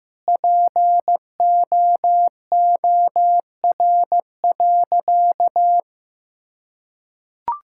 Morse code, heard as a keyed tone, about 15 wpm, 700 Hz. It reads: POOR.
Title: Sentences from Top 1000 Words - Review Part 6 - 15wpm